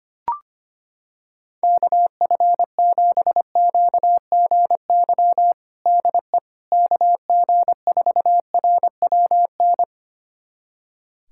Morse code, heard as a keyed tone, about 25 wpm, 700 Hz. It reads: KF7QGY DE KG4RWN
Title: USA Callsign DE Callsign - 6 Characters - 25wpm